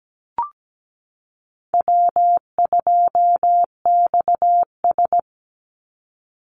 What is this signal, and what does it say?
Morse code, heard as a keyed tone, about 17 wpm, 700 Hz. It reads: W2XS
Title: Easy Call Signs Repeated 3x - 17wpm